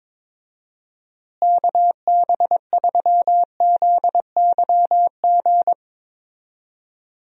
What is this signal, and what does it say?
Morse code, heard as a keyed tone, about 22 wpm, 700 Hz. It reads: KB3ZYG
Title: USA Call Signs - 6 Characters - Repeated 3x - 22wpm